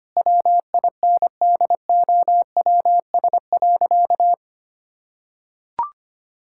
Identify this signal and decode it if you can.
Morse code, heard as a keyed tone, about 25 wpm, 700 Hz. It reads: WINDOWS.
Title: Pangram Challenge v2 - 25wpm